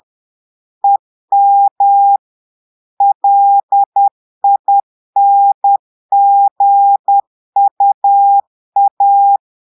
Morse code, heard as a keyed tone, about 10 wpm, 800 Hz. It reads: EM LINGUA
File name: Portugais - texte en clair 1 - vitesse - 10 mots minute